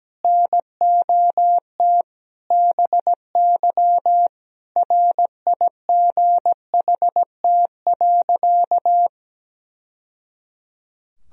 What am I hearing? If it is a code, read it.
Morse code, heard as a keyed tone, about 17 wpm, 700 Hz. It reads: NOT BY RIGHT.
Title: Sentences from Top 1000 Words - Review Part 3 - 17wpm